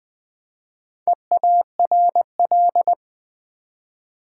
Morse code, heard as a keyed tone, about 20 wpm, 700 Hz. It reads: EARL
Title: Build A Word - 1 Letter At A Time - 5-Letter Words - 20wpm